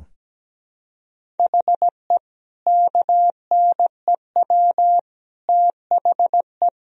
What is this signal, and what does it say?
Morse code, heard as a keyed tone, about 17 wpm, 700 Hz. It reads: HE KNEW THE
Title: Sentences from Top 1000 Words - Review Part 1 - 17wpm